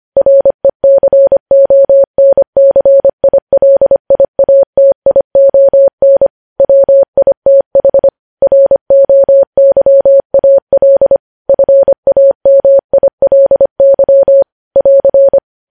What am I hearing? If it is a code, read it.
Morse code, heard as a keyed tone, about 25 wpm, 550 Hz. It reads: RECONCILIATION WITH ROYAL FAMILY <AR>
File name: podcast / News headlines-1x-25x25